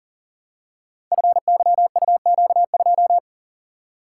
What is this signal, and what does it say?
Morse code, heard as a keyed tone, about 38 wpm, 700 Hz. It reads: FYUQ2
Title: Code Groups - 5 Letter:Numbers - 40wpm